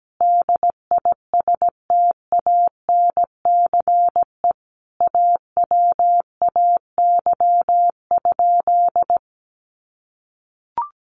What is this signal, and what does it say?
Morse code, heard as a keyed tone, about 17 wpm, 700 Hz. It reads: DISTANCE AWAY?
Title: Sentences from Top 1200 Words 17wpm